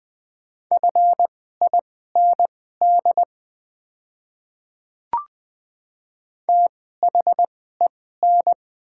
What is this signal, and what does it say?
Morse code, heard as a keyed tone, about 20 wpm, spaced as for 15 wpm, 690 Hz. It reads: FIND THEN
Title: 2x Character Spacing - Top 100 Words - 20wpm